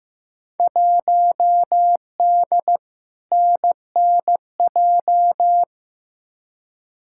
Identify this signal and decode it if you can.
Morse code, heard as a keyed tone, about 15 wpm, 700 Hz. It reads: 1D NNJ
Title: Field Day Exchanges Encore - 15wpm